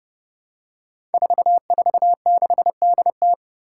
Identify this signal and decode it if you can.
Morse code, heard as a keyed tone, about 32 wpm, 700 Hz. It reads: 446DT
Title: Five Challenging Letter-Number Combinations 30wpm